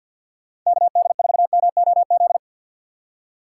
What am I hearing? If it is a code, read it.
Morse code, heard as a keyed tone, about 49 wpm, 690 Hz. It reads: KD4MOZ
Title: USA Call Signs - 6 Characters - 3x Character Spacing - 50wpm